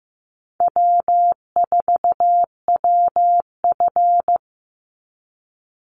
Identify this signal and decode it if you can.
Morse code, heard as a keyed tone, about 15 wpm, 700 Hz. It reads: W4WF
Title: Easy Call Signs Repeated 3x - 15wpm